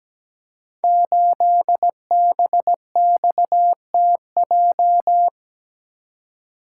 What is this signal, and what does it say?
Morse code, heard as a keyed tone, about 17 wpm, 700 Hz. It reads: 8BXTJ
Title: Code Groups - 5 Letter:Numbers - 17wpm